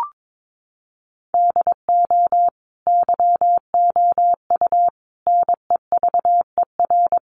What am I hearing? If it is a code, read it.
Morse code, heard as a keyed tone, about 22 wpm, 700 Hz. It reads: DO YOU NEVER
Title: Sentences from Top 1500 Words 22wpm